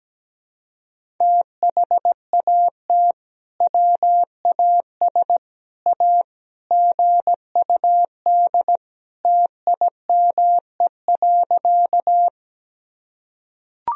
Morse code, heard as a keyed tone, about 17 wpm, 700 Hz. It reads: THAT WAS A GUD TIME.
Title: Sentences from Top 200 Words and Abbreviations 17wpm